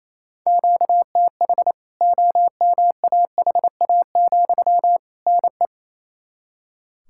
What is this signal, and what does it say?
Morse code, heard as a keyed tone, about 28 wpm, 700 Hz. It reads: QTH OMAHA, NE
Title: K1USN SST QSO Practice 28wpm